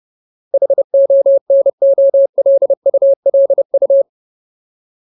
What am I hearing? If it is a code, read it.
Morse code, heard as a keyed tone, about 30 wpm, 550 Hz. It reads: HONOLULU
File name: Ultimate Word List / 30WPM Ultimate